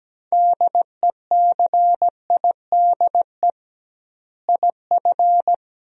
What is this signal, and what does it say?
Morse code, heard as a keyed tone, about 17 wpm, 700 Hz. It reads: DECIDE IF
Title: 2x Word-Spacing - Sentences from Top 500 Words - 17wpm